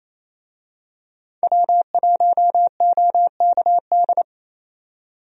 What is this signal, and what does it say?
Morse code, heard as a keyed tone, about 28 wpm, 700 Hz. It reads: W1OKD
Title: USA Call Signs - 5 Characters - 28wpm